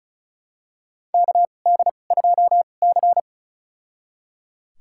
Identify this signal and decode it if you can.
Morse code, heard as a keyed tone, about 34 wpm, spaced as for 25 wpm, 700 Hz. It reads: KD2C
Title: USA Call Signs - 2x Character Spacing - 35wpm